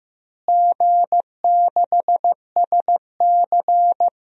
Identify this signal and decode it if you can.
Morse code, heard as a keyed tone, about 15 wpm, 700 Hz. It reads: G6SC
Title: DXCC-100 Call Signs - 4 Characters - Repeated 3x - 15wpm